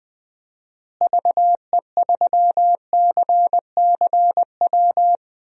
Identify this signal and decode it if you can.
Morse code, heard as a keyed tone, about 20 wpm, 700 Hz. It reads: VE3CCW
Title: DXCC-100 Call Signs - 6 Characters - Repeated 3x - 20wpm